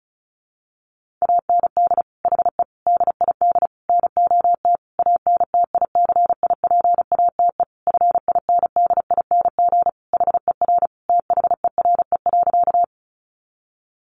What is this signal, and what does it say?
Morse code, heard as a keyed tone, about 34 wpm, 700 Hz. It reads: AND HE DID NOT ANTICIPATE FINDING HER THERE.
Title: Sentences from Top 1500 Words 35wpm